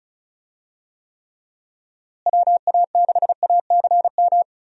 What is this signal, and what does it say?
Morse code, heard as a keyed tone, about 34 wpm, 700 Hz. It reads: WA6ACM
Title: USA Call Signs - 6 Characters - Repeated 3x - 35wpm